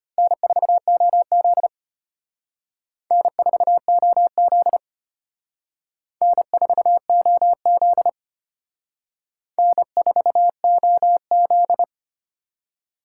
Morse code, heard as a keyed tone, about 32 wpm, 700 Hz. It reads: N4OZ N4OZ N4OZ N4OZ